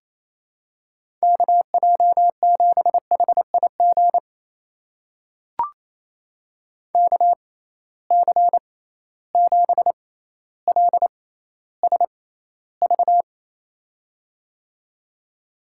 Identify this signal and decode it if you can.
Morse code, heard as a keyed tone, about 28 wpm, spaced as for 14 wpm, 700 Hz. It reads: KJ7HIG KC7LSV
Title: USA Call Signs - 6 Characters - 6x Character Spacing - 28wpm